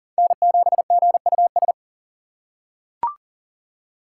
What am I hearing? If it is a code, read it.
Morse code, heard as a keyed tone, about 38 wpm, 700 Hz. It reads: N7GUS
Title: USA Call Signs - 5 Characters - 2x Character Spacing - 40wpm